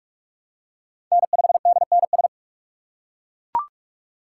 Morse code, heard as a keyed tone, about 41 wpm, 700 Hz. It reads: N5DNS
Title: USA Call Signs - 5 Characters - 4x Character Spacing - 45wpm